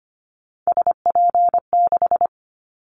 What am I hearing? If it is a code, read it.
Morse code, heard as a keyed tone, about 25 wpm, 700 Hz. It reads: SP6
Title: Callsign Prefixes for Top 25 DXCC - Repeated 3x - 25wpm